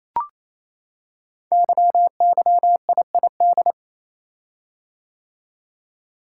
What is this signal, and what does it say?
Morse code, heard as a keyed tone, about 27 wpm, 700 Hz. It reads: YYIID
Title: Code Groups - 5 Letter:Numbers - 28wpm